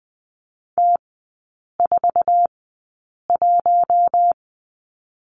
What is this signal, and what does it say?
Morse code, heard as a keyed tone, about 20 wpm, spaced as for 8 wpm, 700 Hz. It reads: T41
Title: 047 Lesson 9e - Number 4 - Three Characters 20wpm